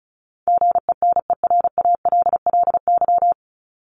Morse code, heard as a keyed tone, about 34 wpm, 700 Hz. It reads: GENERALLY
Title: LY Suffix - Top 100 Words with Base Words 35wpm